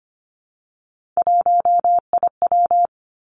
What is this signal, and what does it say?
Morse code, heard as a keyed tone, about 25 wpm, 700 Hz. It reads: 1IW
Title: Three Challenging Letter-Number Combinations 25wpm